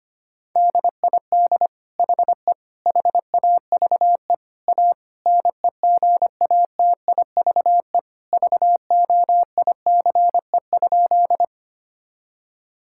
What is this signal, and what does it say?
Morse code, heard as a keyed tone, about 25 wpm, 700 Hz. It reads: DID HE HAVE A NEGATIVE VOICE?
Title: Sentences from Top 1000 Words - Review Part 6 - 25wpm